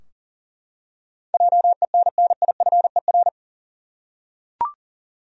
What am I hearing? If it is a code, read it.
Morse code, heard as a keyed tone, about 40 wpm, 700 Hz. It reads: JENNIFER
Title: USA Common Names - 40wpm